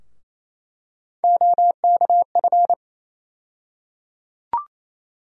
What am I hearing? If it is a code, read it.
Morse code, heard as a keyed tone, about 28 wpm, 700 Hz. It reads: OKF
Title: USA Call Sign Postfixes - 3 Letters 28wpm